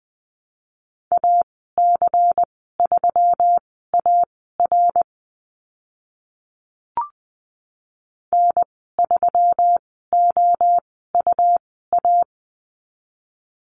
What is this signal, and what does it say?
Morse code, heard as a keyed tone, about 20 wpm, spaced as for 14 wpm, 700 Hz. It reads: AC3AR N3OUA
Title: 128 Lesson 21g - Number 6 - Call Signs 20wpm